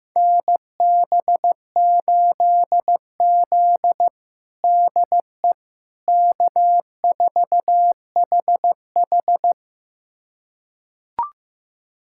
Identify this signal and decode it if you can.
Morse code, heard as a keyed tone, about 15 wpm, 700 Hz. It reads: NB8Z DE K4HH
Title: USA Callsign DE Callsign - 4 Characters - 15wpm